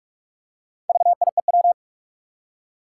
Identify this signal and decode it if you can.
Morse code, heard as a keyed tone, about 46 wpm, 700 Hz. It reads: VIEW
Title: Four Letter Words - 45wpm